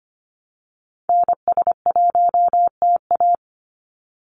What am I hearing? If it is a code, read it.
Morse code, heard as a keyed tone, about 25 wpm, 690 Hz. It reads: NS1TA
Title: NA SOTA Activator Callsigns 25wpm